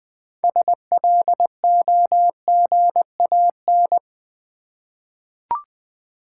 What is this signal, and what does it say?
Morse code, heard as a keyed tone, about 20 wpm, 700 Hz. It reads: SLOGAN
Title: 6 Letter Words - First 3 Letters Given - 20wpm